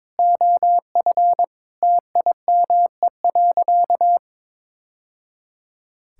Morse code, heard as a keyed tone, about 22 wpm, 700 Hz. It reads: OF TIME.
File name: practice / 22wpm / Sentences with 3-Word Phrases - 22wpm